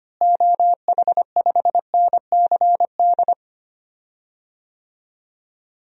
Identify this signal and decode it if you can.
Morse code, heard as a keyed tone, about 25 wpm, 700 Hz. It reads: OH5NCD
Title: DXCC-100 Call Signs - 6 Characters - Repeated 3x - 25wpm